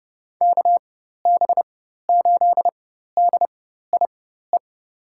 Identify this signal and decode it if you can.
Morse code, heard as a keyed tone, about 30 wpm, spaced as for 14 wpm, 700 Hz. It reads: KB8DIE
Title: USA Call Signs - 6 Characters - 4x Character Spacing - 30wpm